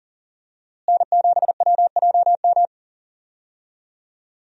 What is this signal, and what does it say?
Morse code, heard as a keyed tone, about 40 wpm, 700 Hz. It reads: N7WJM
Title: USA Call Signs - 5 Characters - Repeated 2x - 40wpm